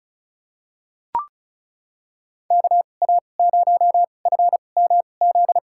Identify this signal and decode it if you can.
Morse code, heard as a keyed tone, about 36 wpm, spaced as for 25 wpm, 700 Hz. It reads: KA0FMZ
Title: USA Call Signs - 6 Characters - 2x Character Spacing - 35wpm